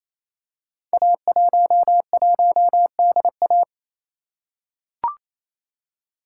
Morse code, heard as a keyed tone, about 29 wpm, 700 Hz. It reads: A11DA
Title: Code Groups - 5 Letter:Numbers - 28wpm